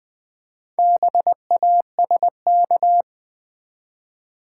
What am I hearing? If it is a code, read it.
Morse code, heard as a keyed tone, about 20 wpm, 700 Hz. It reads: BASK
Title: Build A Word - 2 Letters At A Time - 6-Letter Words - 20wpm